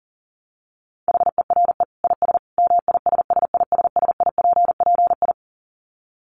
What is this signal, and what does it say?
Morse code, heard as a keyed tone, about 40 wpm, 700 Hz. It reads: HERE IS MISSISSIPPI
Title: Dit Challenge 40wpm